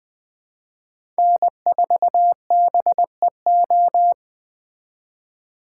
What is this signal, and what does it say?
Morse code, heard as a keyed tone, about 20 wpm, 700 Hz. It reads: N4BEO
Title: Speed Racing - USA 1x3 Call Signs 17wpm